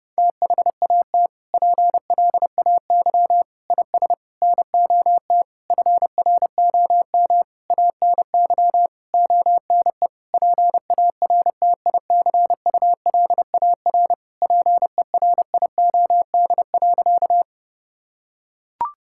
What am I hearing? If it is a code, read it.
Morse code, heard as a keyed tone, about 32 wpm, 700 Hz. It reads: THAT PLAY IS NOT FROM ANY ONE PARTICULAR PERIOD.
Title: Sentences from Top 1000 Words - Review Part 5 - 30wpm